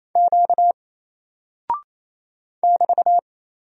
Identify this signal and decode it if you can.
Morse code, heard as a keyed tone, about 29 wpm, 700 Hz. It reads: Q <BT>
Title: Single Character (Letter:Number:Punctuation:Prosign) 28wpm